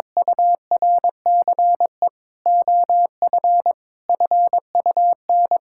Morse code, heard as a keyed tone, about 22 wpm, 700 Hz. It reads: URCE OF FUN
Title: Sentences from Top 1000 Words - Review Part 5 - 22wpm